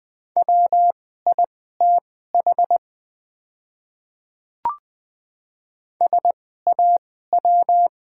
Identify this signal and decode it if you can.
Morse code, heard as a keyed tone, about 20 wpm, spaced as for 14 wpm, 690 Hz. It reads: WITH SAW
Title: 2x Character Spacing - Top 500 Words - 20wpm